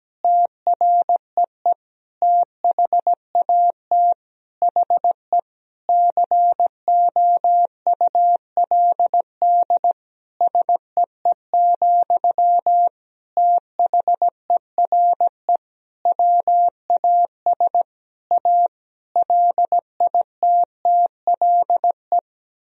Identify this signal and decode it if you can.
Morse code, heard as a keyed tone, about 17 wpm, 700 Hz. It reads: TREE THAT HE COULD SEE, THERE WAS A LITTLE
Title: Sentences from Top 1000 Words - Review Part 1 - 17wpm